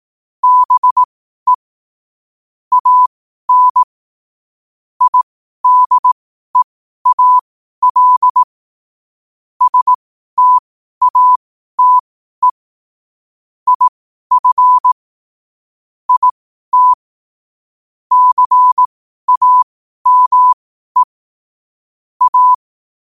Morse code, heard as a keyed tone, about 18 wpm, spaced as for 12 wpm, 1000 Hz. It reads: BE AN IDEAL STATE IF IT CAME A
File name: qotdfiles / 12wpm